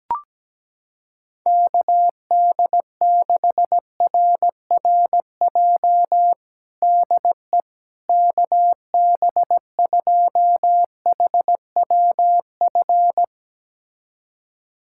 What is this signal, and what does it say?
Morse code, heard as a keyed tone, about 17 wpm, 700 Hz. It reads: KD6RRJ DE KB2HWF
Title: USA Callsign DE Callsign - 6 Characters - 17wpm